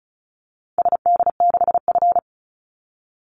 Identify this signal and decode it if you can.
Morse code, heard as a keyed tone, about 34 wpm, 700 Hz. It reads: SD6F
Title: Call Signs Repeated 3x - 35wpm